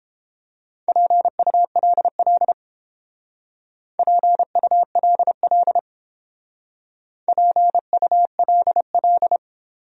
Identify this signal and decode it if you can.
Morse code, heard as a keyed tone, about 31 wpm, 700 Hz. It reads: PULL PULL PULL